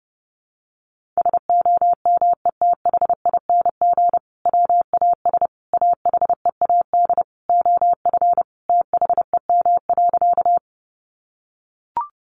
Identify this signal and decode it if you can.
Morse code, heard as a keyed tone, about 30 wpm, 700 Hz. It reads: SOMETHING WAS AHEAD OF THEM.
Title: Sentences from Top 1100 Words 30wpm